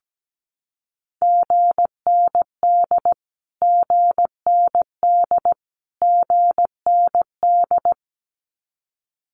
Morse code, heard as a keyed tone, about 17 wpm, 700 Hz. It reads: GND GND GND
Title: Abbreviations Repeated 3x - 17wpm